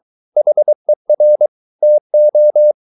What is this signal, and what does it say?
Morse code, heard as a keyed tone, about 23 wpm, 600 Hz. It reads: HER TO